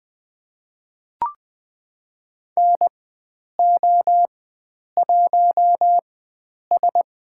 Morse code, heard as a keyed tone, about 20 wpm, spaced as for 9 wpm, 700 Hz. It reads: NO1S
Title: USA Call Signs - 4 Characters - 4x Character Spacing - 20wpm